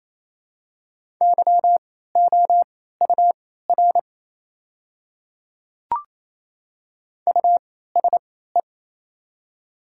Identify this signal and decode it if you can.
Morse code, heard as a keyed tone, about 28 wpm, spaced as for 16 wpm, 690 Hz. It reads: YOUR USE
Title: 3x Character Spacing - Top 500 Words - 28wpm